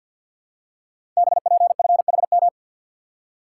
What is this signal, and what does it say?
Morse code, heard as a keyed tone, about 50 wpm, 700 Hz. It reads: BPFHM
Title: Code Groups - 5 Letters - 50wpm